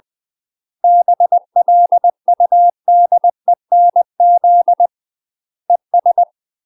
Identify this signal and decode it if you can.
Morse code, heard as a keyed tone, about 20 wpm, 700 Hz. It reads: BLUDENZ ES
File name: UK MASTER QSOs TRAINING / 20WPM / UK MASTER QSOs 0038